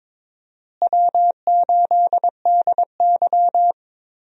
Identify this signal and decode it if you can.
Morse code, heard as a keyed tone, about 22 wpm, 700 Hz. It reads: W8DY
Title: USA Call Signs - 4 Characters - Repeated 2x - 22wpm